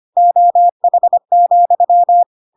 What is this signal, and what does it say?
Morse code, heard as a keyed tone, about 25 wpm, 700 Hz. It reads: OH,
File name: Pinocchio / Pinocchio ch5 25 25 ik1hge